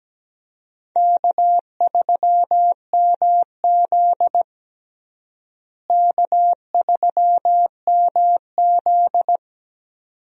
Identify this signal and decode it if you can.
Morse code, heard as a keyed tone, about 17 wpm, 700 Hz. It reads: K3MZ K3MZ
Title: USA Call Signs - 4 Characters - Repeated 2x - 17wpm